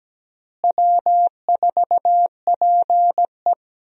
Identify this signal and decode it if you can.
Morse code, heard as a keyed tone, about 17 wpm, 700 Hz. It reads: W4PE
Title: USA Call Signs - 4 Characters - Repeated 3x - 17wpm